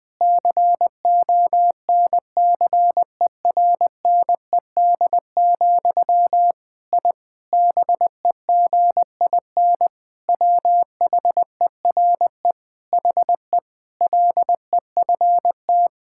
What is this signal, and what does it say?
Morse code, heard as a keyed tone, about 21 wpm, 700 Hz. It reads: CONCERNED, I BEGIN WHERE HE LEFT
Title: Sentences from Top 1000 Words - Review Part 6 - 20wpm